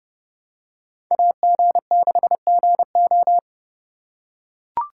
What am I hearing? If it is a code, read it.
Morse code, heard as a keyed tone, about 29 wpm, 700 Hz. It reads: AG6GO